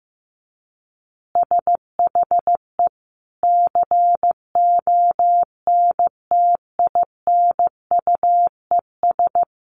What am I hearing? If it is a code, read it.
Morse code, heard as a keyed tone, about 14 wpm, 700 Hz. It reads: SHE CONTINUES
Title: Easy Phrases 15wpm